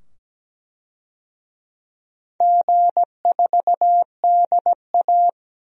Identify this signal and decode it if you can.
Morse code, heard as a keyed tone, about 17 wpm, 700 Hz. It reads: G4DA
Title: DXCC-100 Call Signs - 4 Characters - 17wpm